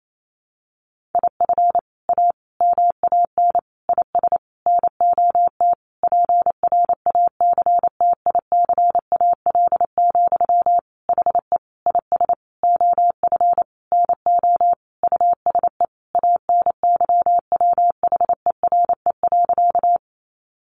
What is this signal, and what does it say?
Morse code, heard as a keyed tone, about 28 wpm, 700 Hz. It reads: IF A MAN IS NOT PRACTICAL, HE IS OF NO USE ANYWHERE.
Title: Sentences from Top 1400 Words 28wpm